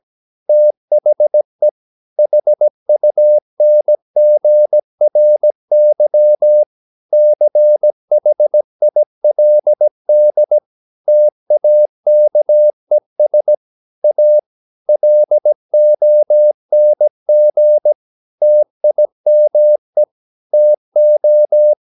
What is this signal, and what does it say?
Morse code, heard as a keyed tone, about 17 wpm, 600 Hz. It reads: THE HUNGRY CHILD TAKES A LONG TIME TO
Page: Romanen Never af Ken Follet på engelsk.